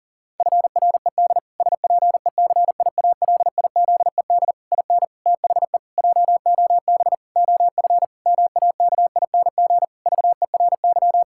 Morse code, heard as a keyed tone, about 40 wpm, 700 Hz. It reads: FRED SPECIALIZED IN THE JOB OF MAKING VERY
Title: Pangram Challenge v2 - 40wpm